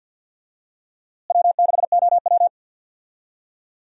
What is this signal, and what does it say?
Morse code, heard as a keyed tone, about 47 wpm, 700 Hz. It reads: W6OW
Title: USA Call Signs - 4 Characters - Repeated 3x - 50wpm